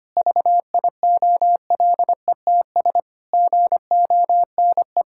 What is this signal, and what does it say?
Morse code, heard as a keyed tone, about 25 wpm, 700 Hz. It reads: VIOLETS GONE
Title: Pangram Challenge v2 - 25wpm